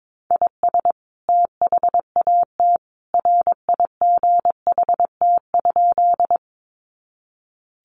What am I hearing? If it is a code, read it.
Morse code, heard as a keyed tone, about 22 wpm, 700 Hz. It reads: IS THAT RIGHT?
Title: Sentences from Top 200 Words 22wpm